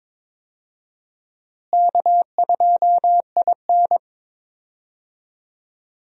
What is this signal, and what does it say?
Morse code, heard as a keyed tone, about 22 wpm, 700 Hz. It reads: K2IN
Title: USA Call Signs - 4 Characters - Repeated 2x - 22wpm